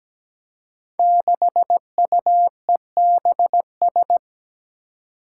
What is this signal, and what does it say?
Morse code, heard as a keyed tone, about 17 wpm, 700 Hz. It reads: 6UEBS
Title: Five Challenging Letter-Number Combinations 17wpm